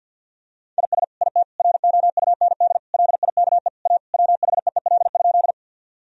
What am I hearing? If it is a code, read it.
Morse code, heard as a keyed tone, about 50 wpm, 700 Hz. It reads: IS IT ROUND LIKE A WHEEL?
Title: Sentences from Top 1000 Words - Review Part 3 - 50wpm